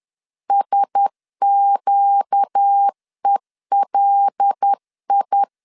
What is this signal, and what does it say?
Morse code, heard as a keyed tone, about 10 wpm, 800 Hz. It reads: SQELI
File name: LEZIONI / 6 PKX0 / 6 P 10wpm 2432